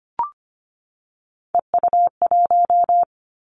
Morse code, heard as a keyed tone, about 25 wpm, 700 Hz. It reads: EU1
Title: Callsign Prefixes for Top 50 DXCC - 25wpm